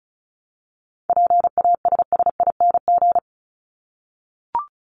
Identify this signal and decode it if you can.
Morse code, heard as a keyed tone, about 34 wpm, 700 Hz. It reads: PASSING
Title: ING Suffix - Top 100 Words with Base Words 35wpm